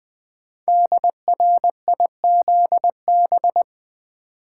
Morse code, heard as a keyed tone, about 20 wpm, 700 Hz. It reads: DRIZB
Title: Code Groups - 5 Letters - 20wpm